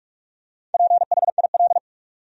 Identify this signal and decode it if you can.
Morse code, heard as a keyed tone, about 43 wpm, 700 Hz. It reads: PHIL
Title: USA Common Names - Max 4 Letters - 45wpm